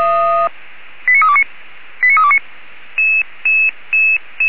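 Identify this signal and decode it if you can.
Five-tone selective call (ZVEI), heard as quick runs of five tones, tones of about 70 ms. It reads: 89329 89329
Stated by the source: ZVEI five-tone call 89324, received on SDR.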